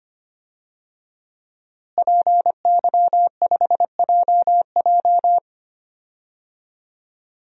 Morse code, heard as a keyed tone, about 25 wpm, 700 Hz. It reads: PY5JJ
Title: DXCC-100 Call Signs - 5 Characters - Repeated 2x - 25wpm